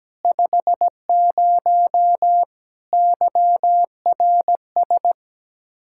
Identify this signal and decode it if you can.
Morse code, heard as a keyed tone, about 17 wpm, 700 Hz. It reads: 50 YRS
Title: Practice QSO - Exchange 3 - More Info - 17wpm